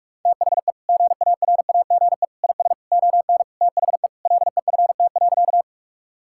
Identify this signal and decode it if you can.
Morse code, heard as a keyed tone, about 46 wpm, 700 Hz. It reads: THE GARAGE IS ON THE LEFT.
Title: Sentences from Top 1600 Words 45wpm